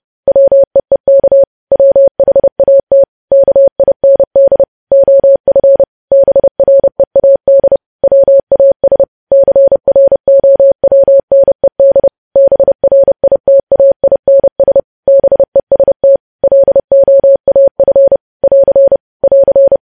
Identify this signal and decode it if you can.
Morse code, heard as a keyed tone, about 30 wpm, 550 Hz. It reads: WEEK WHAT KIND OF BREAD WAS CROWNED BRITAINS BEST LOAF <AR> <AR>
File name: podcast / News headlines-1x-30x30